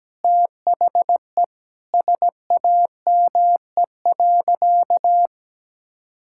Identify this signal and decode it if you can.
Morse code, heard as a keyed tone, about 17 wpm, 700 Hz. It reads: THE SAME.
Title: Sentences from Top 1000 Words - Review Part 5 - 17wpm